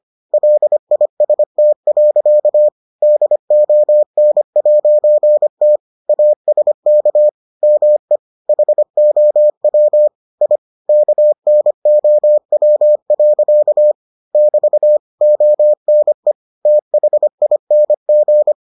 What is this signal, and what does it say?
Morse code, heard as a keyed tone, about 25 wpm, 600 Hz. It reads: LIST. DON'T ASK ME HOW I KNOW. <BT> ONE THING